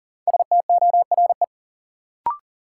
Morse code, heard as a keyed tone, about 40 wpm, 700 Hz. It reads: STORE
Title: Sentences from Top 1000 Words - Review Part 5 - New Word Drill - 40wpm